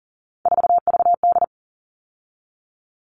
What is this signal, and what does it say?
Morse code, heard as a keyed tone, about 40 wpm, 700 Hz. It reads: 4VD
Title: All Cut Numbers - 3-Digit 40wpm